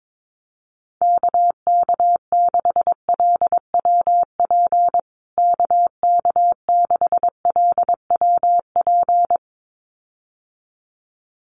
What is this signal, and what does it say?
Morse code, heard as a keyed tone, about 22 wpm, 700 Hz. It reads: KK6LWP KK6LWP
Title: USA Call Signs Repeated 2x 22wpm